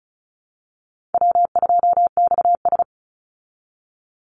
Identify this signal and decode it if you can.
Morse code, heard as a keyed tone, about 34 wpm, 700 Hz. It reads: W2XS
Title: Easy Call Signs Repeated 3x - 35wpm